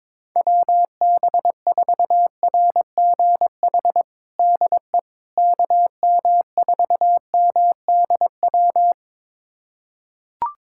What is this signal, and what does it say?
Morse code, heard as a keyed tone, about 22 wpm, 700 Hz. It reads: WB4RGH DE KM4MDW
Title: USA Callsign DE Callsign 22wpm